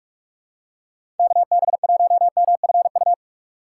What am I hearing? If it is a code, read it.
Morse code, heard as a keyed tone, about 46 wpm, 700 Hz. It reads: KB1MFU
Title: USA Call Signs - 6 Characters - 3x Character Spacing - 45wpm